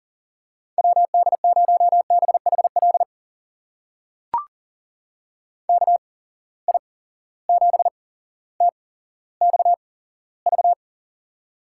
Morse code, heard as a keyed tone, about 42 wpm, spaced as for 17 wpm, 700 Hz. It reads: WD0BHL KI7TXV
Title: USA Call Signs - 6 Characters - 8x Character Spacing - 40wpm